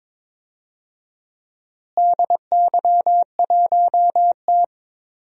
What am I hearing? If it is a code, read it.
Morse code, heard as a keyed tone, about 21 wpm, 700 Hz. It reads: DY1T
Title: DXCC-100 Call Signs - 4 Characters - Repeated 3x - 22wpm